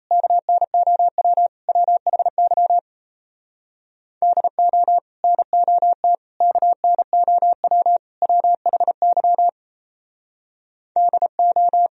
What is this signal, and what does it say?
Morse code, heard as a keyed tone, about 36 wpm, 690 Hz. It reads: KNOW WHY DO NOT KNOW WHY DO